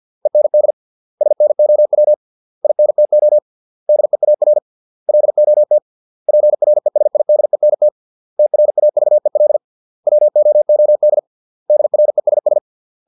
Morse code, heard as a keyed tone, about 50 wpm, 600 Hz. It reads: END SNOW INTO DEAR LOT PRESIDENT TRAVEL WOOD DRESS